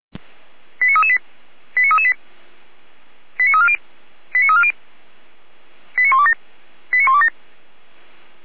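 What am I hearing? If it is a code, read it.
Five-tone selective call (ZVEI), heard as quick runs of five tones, tones of about 70 ms. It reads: 89338 89338 89340 89340 89127 89127
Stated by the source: Five-tone ZVEI selective call received on an SDR, recorded by feuerwehr-orlamuende.org (fire brigade Orlamünde).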